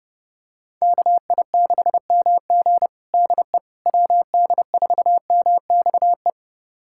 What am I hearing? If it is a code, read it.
Morse code, heard as a keyed tone, about 30 wpm, 700 Hz. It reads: KI6MG DE WD4MXE
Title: USA Callsign DE Callsign 30wpm